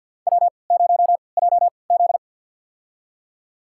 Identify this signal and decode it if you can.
Morse code, heard as a keyed tone, about 50 wpm, spaced as for 28 wpm, 700 Hz. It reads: W0JZ
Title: USA Call Signs - 4 Characters - 3x Character Spacing - 50wpm